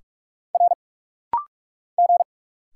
Morse code, heard as a keyed tone, about 46 wpm, 700 Hz. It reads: R G
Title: Single Letter-Number - Rapid-Fire 45wpm